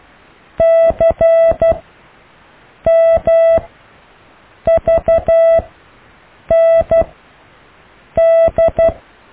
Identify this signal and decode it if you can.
Morse code, heard as a keyed tone, about 12 wpm, spaced as for 6 wpm, 660 Hz. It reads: CMVND
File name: cw / IJLUM+CDRO+EKTPV+BFHNW